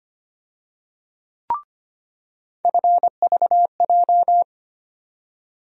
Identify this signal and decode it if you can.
Morse code, heard as a keyed tone, about 25 wpm, 700 Hz. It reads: FVJ
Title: USA Call Sign Postfixes - 3 Letters 25wpm